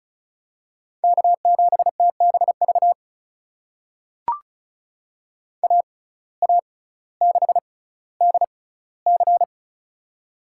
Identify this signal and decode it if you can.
Morse code, heard as a keyed tone, about 36 wpm, spaced as for 18 wpm, 700 Hz. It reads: K7TBV AA6DC
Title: USA Call Signs - 5 Characters - 6x Character Spacing - 35wpm